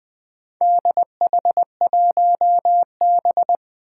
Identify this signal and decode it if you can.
Morse code, heard as a keyed tone, about 20 wpm, 700 Hz. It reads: DH1B
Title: Four Challenging Letter-Number Combinations 20wpm